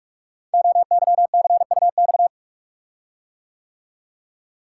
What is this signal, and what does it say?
Morse code, heard as a keyed tone, about 44 wpm, 700 Hz. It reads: OYCUX
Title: Code Groups - 5 Letter:Numbers - 45wpm